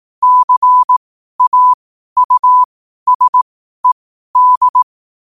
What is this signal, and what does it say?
Morse code, heard as a keyed tone, about 18 wpm, spaced as for 12 wpm, 1000 Hz. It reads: CAUSED
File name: qotdfiles / 12wpm